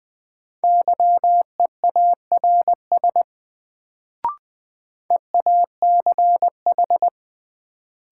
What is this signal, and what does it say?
Morse code, heard as a keyed tone, about 20 wpm, 700 Hz. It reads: YEARS EACH
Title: Sets of 2 Words - New Word Drill - 20wpm